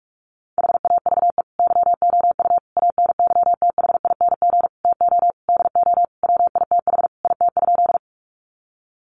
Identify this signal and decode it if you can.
Morse code, heard as a keyed tone, about 44 wpm, 700 Hz. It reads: HAVE YOU ANYTHING TO DO WITH IT?
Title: Sentences from Top 1000 Words 45wpm